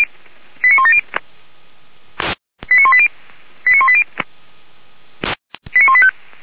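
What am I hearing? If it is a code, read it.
Five-tone selective call (ZVEI), heard as quick runs of five tones, tones of about 80 ms. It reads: 89170 89180 89180 89185
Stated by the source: ZVEI five-tone call received on SDR.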